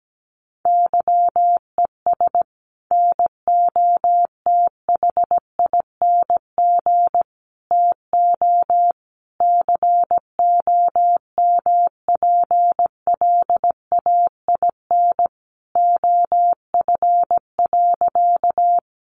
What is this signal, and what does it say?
Morse code, heard as a keyed tone, about 17 wpm, 700 Hz. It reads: YES NOTHING TO COMPLAIN OF.
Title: Sentences from Top 1300 Words 17wpm